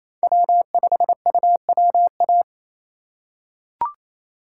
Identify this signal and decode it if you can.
Morse code, heard as a keyed tone, about 28 wpm, 700 Hz. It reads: W5UWA
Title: USA Call Signs - 2x Character Spacing - 28wpm